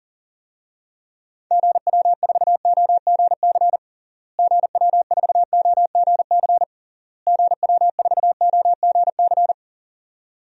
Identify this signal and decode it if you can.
Morse code, heard as a keyed tone, about 37 wpm, 700 Hz. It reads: GW4OGC GW4OGC GW4OGC
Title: DXCC-100 Call Signs - 6 Characters - Repeated 3x - 40wpm